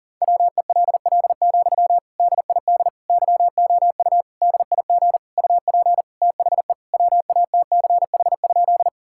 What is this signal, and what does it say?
Morse code, heard as a keyed tone, about 40 wpm, 700 Hz. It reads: WELL, DID YOU DIG UP THE WATCH?
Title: Sentences from Top 1600 Words 40wpm